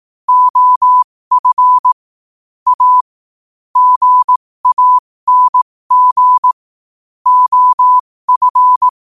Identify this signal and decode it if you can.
Morse code, heard as a keyed tone, about 18 wpm, spaced as for 15 wpm, 1000 Hz. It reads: OF A GANG OF
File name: qotdfiles / 15wpm